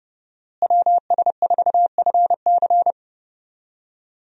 Morse code, heard as a keyed tone, about 30 wpm, 700 Hz. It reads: WS4FC
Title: USA Call Signs - 5 Characters - Repeated 3x - 30wpm